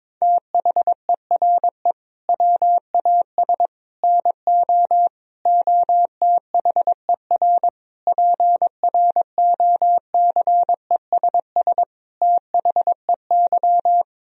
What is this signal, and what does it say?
Morse code, heard as a keyed tone, about 22 wpm, 700 Hz. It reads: THERE WAS NO OTHER PROCESS THEY
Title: Sentences from Top 1000 Words - Review Part 3 - 22wpm